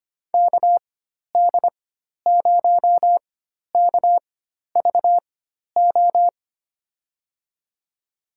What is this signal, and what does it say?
Morse code, heard as a keyed tone, about 25 wpm, spaced as for 12 wpm, 700 Hz. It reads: KD0KVO
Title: USA Call Signs - 6 Characters - 4x Character Spacing - 25wpm